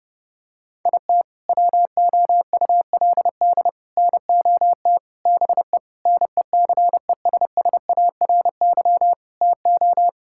Morse code, heard as a keyed tone, about 30 wpm, 700 Hz. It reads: IT WOULD NOT BE NECESSARY TO
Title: Sentences from Top 1600 Words 30wpm